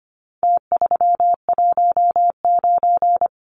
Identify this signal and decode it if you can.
Morse code, heard as a keyed tone, about 25 wpm, 700 Hz. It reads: T319
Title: Numbers - Contest Style - 4-Digit 25wpm